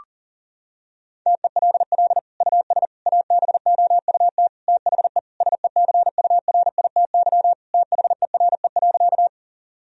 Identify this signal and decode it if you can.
Morse code, heard as a keyed tone, about 40 wpm, 700 Hz. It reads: TELL US ABOUT THE SECURITY THERE.
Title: Sentences from Top 1000 Words - Review Part 4 - 40wpm